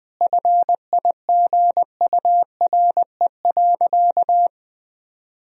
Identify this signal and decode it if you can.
Morse code, heard as a keyed tone, about 20 wpm, 700 Hz. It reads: FIGURE.
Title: Sentences from Top 1000 Words - Review Part 2 - 20wpm